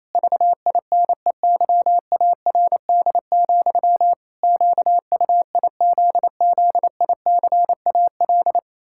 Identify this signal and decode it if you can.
Morse code, heard as a keyed tone, about 28 wpm, 700 Hz. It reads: VINEYARD, QUIZZICAL
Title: Pangram Challenge v2 - 28wpm